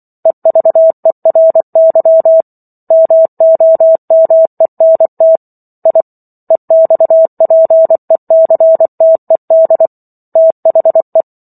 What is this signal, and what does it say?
Morse code, heard as a keyed tone, about 24 wpm, 650 Hz. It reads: EVERY MOMENT I EXPECTED THE
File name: Book One / 15-Chapter 15 - 24 WPM